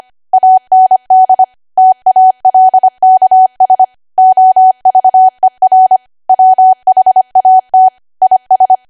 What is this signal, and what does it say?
Morse code, heard as a keyed tone, about 25 wpm, 750 Hz. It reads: AND TALKS OVER WHAT IS
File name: Archive / 25 WPM / 25WPM